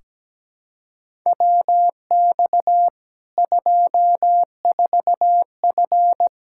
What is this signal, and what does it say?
Morse code, heard as a keyed tone, about 17 wpm, 700 Hz. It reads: WX 24F
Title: Practice QSO - Exchange 2 - Info - 17wpm